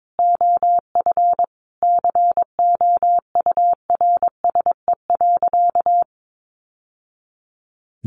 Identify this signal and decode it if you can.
Morse code, heard as a keyed tone, about 22 wpm, 700 Hz. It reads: OF COURSE.
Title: Sentences from Top 1100 Words 22wpm